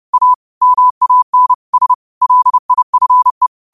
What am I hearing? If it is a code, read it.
Morse code, heard as a keyed tone, about 30 wpm, 1000 Hz. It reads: A MAN S LIFE
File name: qotdfiles / 30wpm